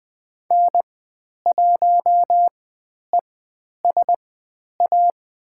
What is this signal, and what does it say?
Morse code, heard as a keyed tone, about 20 wpm, spaced as for 10 wpm, 700 Hz. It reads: N1ESA
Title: 109 Lesson 18g - Letter m - Call Signs 20wpm